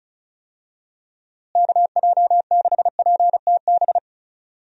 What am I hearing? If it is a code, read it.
Morse code, heard as a keyed tone, about 36 wpm, 700 Hz. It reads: KJ6PTB
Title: USA Call Signs - 6 Characters - Repeated 3x - 35wpm